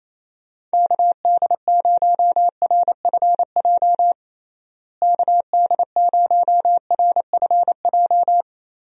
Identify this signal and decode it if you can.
Morse code, heard as a keyed tone, about 28 wpm, 700 Hz. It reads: KD0RFJ KD0RFJ
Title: USA Call Signs - 6 Characters - Repeated 3x - 28wpm